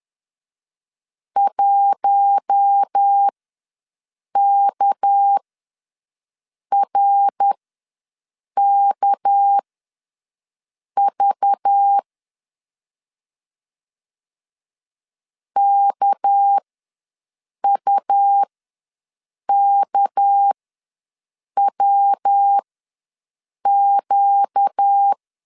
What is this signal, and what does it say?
Morse code, heard as a keyed tone, about 10 wpm, spaced as for 6 wpm, 800 Hz. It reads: 1KRKV KUKWQ
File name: LEZIONI / 6 PKX0 / 6 PK ws 10wpm 3745